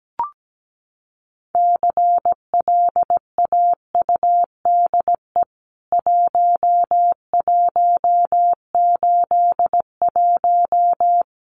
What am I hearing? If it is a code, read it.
Morse code, heard as a keyed tone, about 17 wpm, 700 Hz. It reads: CLAUDE 1181
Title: CWT Exchange 17wpm